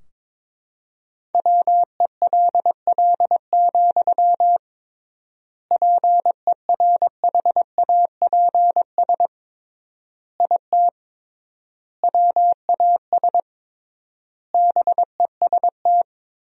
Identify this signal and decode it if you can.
Morse code, heard as a keyed tone, about 22 wpm, 700 Hz. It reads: WELL, PERHAPS IT WAS BEST
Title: 3x Word-Spacing - Sentences from Top 500 Words - 22wpm